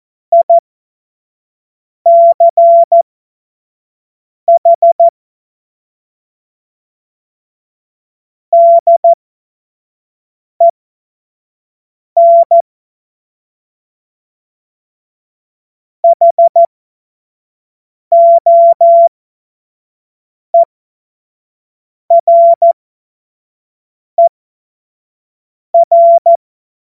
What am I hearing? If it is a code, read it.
Morse code, heard as a keyed tone, about 14 wpm, spaced as for 5 wpm, 680 Hz. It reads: ICH DEN HOERER
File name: Morsekurs / Witze / Witz 03. Januar 2023 25 F70